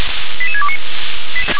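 Five-tone selective call (ZVEI), heard as quick runs of five tones, tones of about 70 ms. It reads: 88629
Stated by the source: Five-tone ZVEI selective call received on an SDR.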